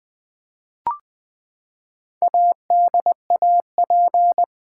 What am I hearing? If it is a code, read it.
Morse code, heard as a keyed tone, about 20 wpm, 700 Hz. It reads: ADAP
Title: Build A Word - 4 Letters At A Time - 10-Letter Words - 20wpm